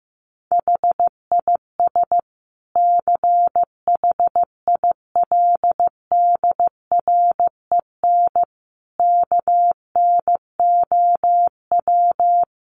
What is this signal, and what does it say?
Morse code, heard as a keyed tone, about 15 wpm, 700 Hz. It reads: HIS CHILDREN KNOW
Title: Sentences from Top 900 Words 15wpm